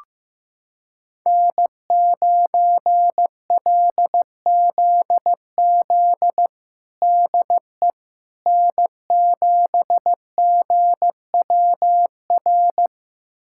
Morse code, heard as a keyed tone, about 15 wpm, 700 Hz. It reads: N9LZZ DE N7GWR
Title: USA Callsign DE Callsign - 5 Characters - 15wpm